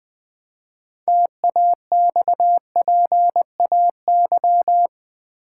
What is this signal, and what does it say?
Morse code, heard as a keyed tone, about 20 wpm, 700 Hz. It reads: TAXPAY
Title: Build A Word - 3 Letters At A Time - 8-Letter Words - 20wpm